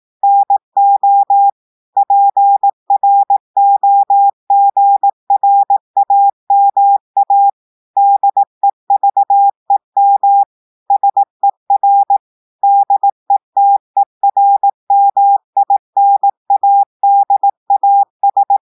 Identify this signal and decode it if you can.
Morse code, heard as a keyed tone, about 18 wpm, 800 Hz. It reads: NO PROGRAMA DEVEM SER DETERMINADAS
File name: portugais texte en clair 2 vitesse 18 mots minutes